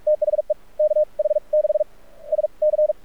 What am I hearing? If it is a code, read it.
Morse code, heard as a keyed tone, about 44 wpm, spaced as for 29 wpm, 610 Hz. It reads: THE KH6SC
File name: archived / CRSnet